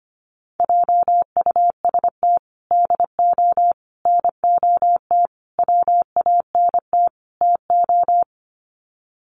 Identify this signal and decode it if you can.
Morse code, heard as a keyed tone, about 25 wpm, 700 Hz. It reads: JUST DO NOT WANT TO
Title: Sets of 5 Words Encore 25wpm